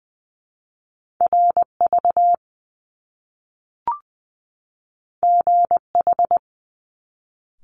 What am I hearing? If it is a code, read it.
Morse code, heard as a keyed tone, about 20 wpm, 700 Hz. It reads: RV GH
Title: Top Two Letter Combinations in Words 20wpm